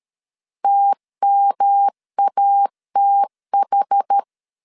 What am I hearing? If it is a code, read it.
Morse code, heard as a keyed tone, about 13 wpm, 800 Hz. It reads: TMATH
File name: CW / DET / 1 3 1500C S B 12wpm 2828